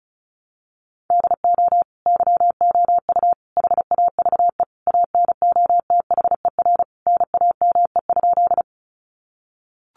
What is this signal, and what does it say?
Morse code, heard as a keyed tone, about 34 wpm, 700 Hz. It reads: DO YOU HAVE ANOTHER NAME?
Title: Sentences from Top 600 Words 35wpm